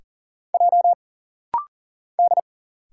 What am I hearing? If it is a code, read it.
Morse code, heard as a keyed tone, about 40 wpm, 700 Hz. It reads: J D
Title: Single Letters - Rapid-Fire 40wpm